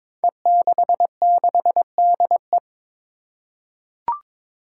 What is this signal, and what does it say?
Morse code, heard as a keyed tone, about 22 wpm, 700 Hz. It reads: E66DE
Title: Five Challenging Letter-Number Combinations 22wpm